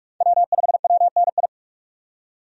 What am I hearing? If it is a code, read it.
Morse code, heard as a keyed tone, about 46 wpm, 700 Hz. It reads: W5WNI
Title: USA Call Signs - 5 Characters - 45wpm